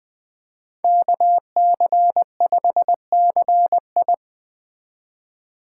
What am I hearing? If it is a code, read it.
Morse code, heard as a keyed tone, about 20 wpm, 700 Hz. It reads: KC5CI
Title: Speed Racing - USA 2x2 Call Signs 17wpm